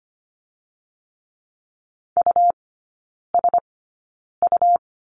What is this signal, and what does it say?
Morse code, heard as a keyed tone, about 25 wpm, spaced as for 9 wpm, 700 Hz. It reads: USU
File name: Lesson 17e - Letter u - Three Characters 25wpm